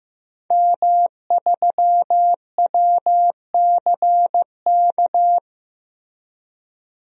Morse code, heard as a keyed tone, about 16 wpm, 700 Hz. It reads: M3WCK
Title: DXCC-100 Call Signs - 5 Characters - Repeated 2x - 15wpm